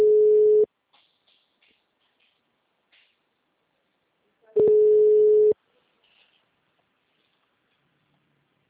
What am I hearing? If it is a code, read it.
Ringback tone (Europe, 425 Hz).